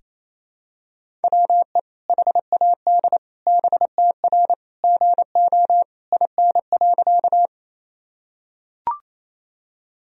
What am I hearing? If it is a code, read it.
Morse code, heard as a keyed tone, about 28 wpm, 700 Hz. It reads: WE HAD BTR GO IN.
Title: Sentences from Top 400 Words and Abbreviations 28wpm